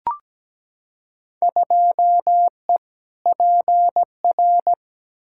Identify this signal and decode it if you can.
Morse code, heard as a keyed tone, about 17 wpm, 700 Hz. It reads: 2E PR
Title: Field Day Exchanges Encore - 17wpm